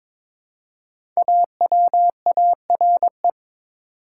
Morse code, heard as a keyed tone, about 22 wpm, 700 Hz. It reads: AWARE
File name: 5 Letter Words - First 3 Letters Given - 22wpm